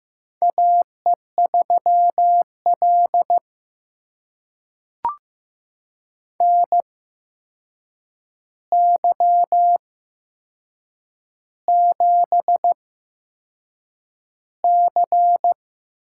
Morse code, heard as a keyed tone, about 15 wpm, spaced as for 6 wpm, 700 Hz. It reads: AE3L NY7C
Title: USA Call Signs - 4 Characters - 8x Character Spacing - 15wpm